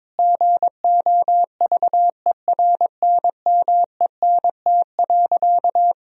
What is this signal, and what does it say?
Morse code, heard as a keyed tone, about 22 wpm, 700 Hz. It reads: GOVERNMENT.
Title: Sentences from Top 1000 Words - Review Part 4 - 22wpm